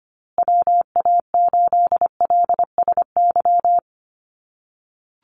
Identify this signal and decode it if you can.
Morse code, heard as a keyed tone, about 25 wpm, 700 Hz. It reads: WA8LSY
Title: USA Call Signs Repeated 3x 25wpm